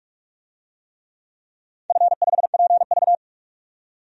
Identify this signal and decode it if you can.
Morse code, heard as a keyed tone, about 44 wpm, 690 Hz. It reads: F5PV
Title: DXCC-100 Call Signs - 4 Characters - 45wpm